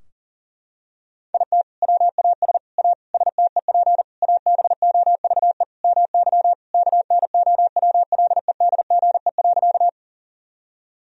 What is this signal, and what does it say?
Morse code, heard as a keyed tone, about 40 wpm, 700 Hz. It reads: IT WAS A STEP ABOVE MY KNOWLEDGE.
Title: Sentences from Top 1000 Words - Review Part 3 - 40wpm